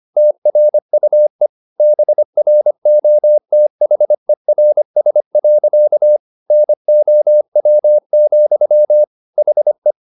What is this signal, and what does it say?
Morse code, heard as a keyed tone, about 25 wpm, 600 Hz. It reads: TRUE BROTHERS. NOW, HE